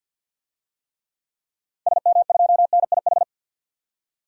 Morse code, heard as a keyed tone, about 52 wpm, 700 Hz. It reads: SM2NIH
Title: DXCC-100 Call Signs - 6 Characters - 50wpm